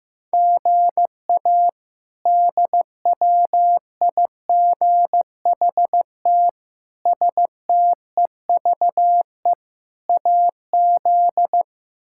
Morse code, heard as a keyed tone, about 15 wpm, 700 Hz. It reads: GA DWIGHT STEVE AZ
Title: K1USN SST QSO Practice - 15wpm